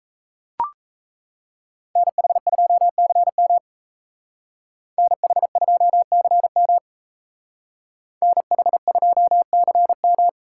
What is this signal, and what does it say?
Morse code, heard as a keyed tone, about 35 wpm, 700 Hz. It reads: NH2CM NH2CM NH2CM